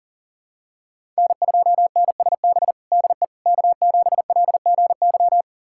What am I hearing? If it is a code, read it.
Morse code, heard as a keyed tone, about 40 wpm, 700 Hz. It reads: N2NSB DE K7LGY
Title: USA Callsign DE Callsign - 5 Characters - 40wpm